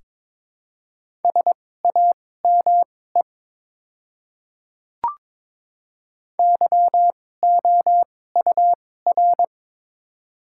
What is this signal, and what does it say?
Morse code, heard as a keyed tone, about 21 wpm, spaced as for 16 wpm, 700 Hz. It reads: SAME YOUR
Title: 2x Character Spacing - Top 300 Words - 22wpm